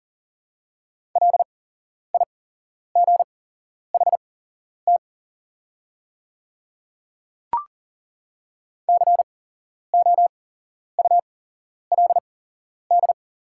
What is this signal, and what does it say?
Morse code, heard as a keyed tone, about 38 wpm, spaced as for 11 wpm, 700 Hz. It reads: LIGHT COULD
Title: 8x Character Spacing - Top 300 Words - 40wpm